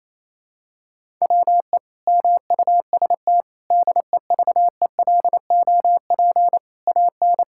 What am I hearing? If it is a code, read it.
Morse code, heard as a keyed tone, about 28 wpm, 700 Hz. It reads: WE MUST DEVELOP AN
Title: Sentences from Top 1000 Words - Review Part 5 - 28wpm